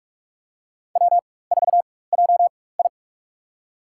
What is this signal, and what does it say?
Morse code, heard as a keyed tone, about 44 wpm, spaced as for 21 wpm, 700 Hz. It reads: W4JI
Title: USA Call Signs - 4 Characters - 4x Character Spacing - 45wpm